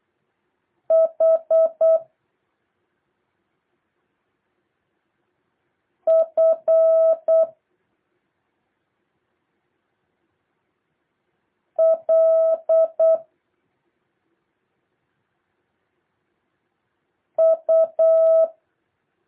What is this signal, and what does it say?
Morse code, heard as a keyed tone, about 8 wpm, 650 Hz. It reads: H F L U
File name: jh8cbh / 2023cw